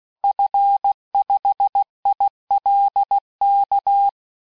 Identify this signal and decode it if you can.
Morse code, heard as a keyed tone, about 16 wpm, 800 Hz. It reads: F5ILK
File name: cours-du-lundi files / cours-1580-fin-QSO